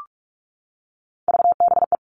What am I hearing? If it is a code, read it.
Morse code, heard as a keyed tone, about 44 wpm, 700 Hz. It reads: VBE
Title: All Cut Numbers - 3-Digit 45wpm